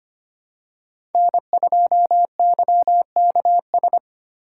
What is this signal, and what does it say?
Morse code, heard as a keyed tone, about 25 wpm, 700 Hz. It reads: N2YKS
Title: USA Call Signs - 5 Characters - 4x Character Spacing - 25wpm